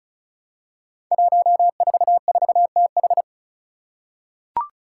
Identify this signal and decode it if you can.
Morse code, heard as a keyed tone, about 36 wpm, 700 Hz. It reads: 144TH
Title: Five Challenging Letter-Number Combinations 35wpm